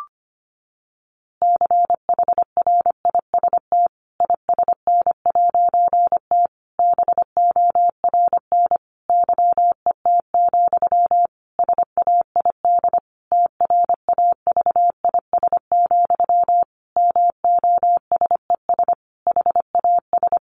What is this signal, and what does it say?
Morse code, heard as a keyed tone, about 25 wpm, 700 Hz. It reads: CHRIST ISN'T BORN YET, SAID TRAVIS, MOSES HAS
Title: A Sound Of Thunder 25wpm